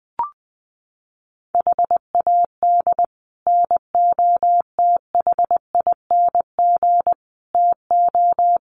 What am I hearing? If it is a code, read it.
Morse code, heard as a keyed tone, about 20 wpm, 700 Hz. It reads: HAD NOTHING TO
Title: Sets of 4 Words 20wpm